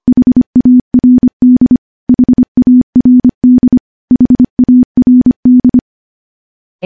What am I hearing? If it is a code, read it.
Morse code, heard as a keyed tone, about 25 wpm, 250 Hz. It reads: HARD HARD HARD
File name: morse-training-25WPM